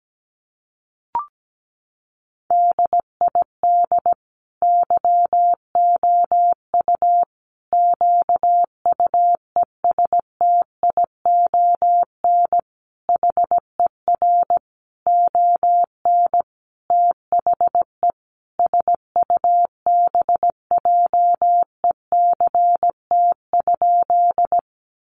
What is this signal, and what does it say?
Morse code, heard as a keyed tone, about 17 wpm, 700 Hz. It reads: DID YOU QUESTION HER ON THE SUBJECT?
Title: Sentences from Top 800 Words 17wpm